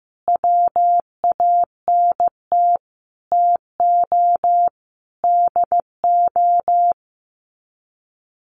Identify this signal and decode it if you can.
Morse code, heard as a keyed tone, about 15 wpm, 700 Hz. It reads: WANT TO DO
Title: Sets of 5 Words 15wpm